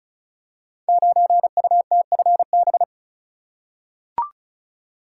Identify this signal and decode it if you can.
Morse code, heard as a keyed tone, about 36 wpm, 700 Hz. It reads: 9UTFB
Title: Code Groups - 5 Letter:Numbers - 35wpm